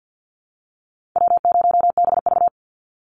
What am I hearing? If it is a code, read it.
Morse code, heard as a keyed tone, about 50 wpm, 700 Hz. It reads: W0BV
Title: Easy Call Signs Repeated 2x - 50wpm